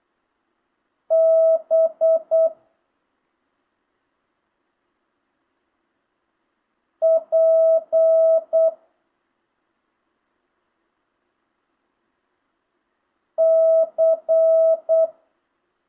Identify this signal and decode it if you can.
Morse code, heard as a keyed tone, about 8 wpm, 650 Hz. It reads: B P C
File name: jh8cbh / 2025cw